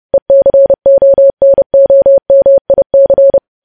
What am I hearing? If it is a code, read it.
Morse code, heard as a keyed tone, about 30 wpm, 550 Hz. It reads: ECONOMIC
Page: news headlines in morse code at 30 wpm